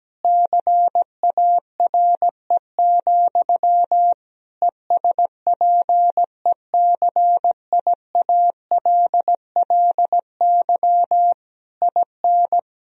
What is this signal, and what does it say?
Morse code, heard as a keyed tone, about 17 wpm, 700 Hz. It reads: CARE, ESPECIALLY IN
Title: Sentences from Top 1000 Words - Review Part 4 - 17wpm